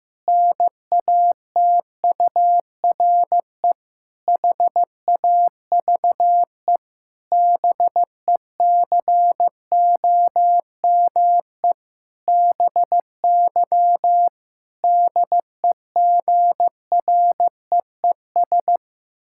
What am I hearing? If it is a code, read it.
Morse code, heard as a keyed tone, about 15 wpm, 700 Hz. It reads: NATURE HAVE BECOME BY DEGREES
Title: Pangram Challenge v2 - 15wpm